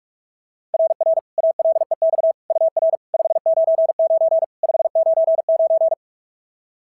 Morse code, heard as a keyed tone, about 46 wpm, 690 Hz. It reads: RR ALEX UR 599 599
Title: SOTA QSO Practice 45wpm